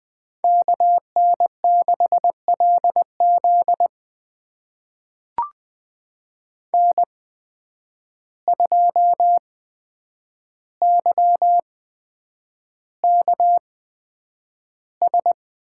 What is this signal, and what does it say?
Morse code, heard as a keyed tone, about 20 wpm, spaced as for 9 wpm, 700 Hz. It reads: KN6LZ N2YKS
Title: USA Call Signs - 5 Characters - 8x Character Spacing - 20wpm